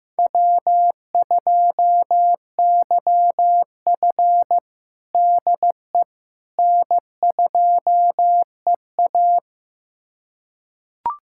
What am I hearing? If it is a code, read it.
Morse code, heard as a keyed tone, about 15 wpm, 700 Hz. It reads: W2YF DE N2EA
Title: USA Callsign DE Callsign - 4 Characters - 15wpm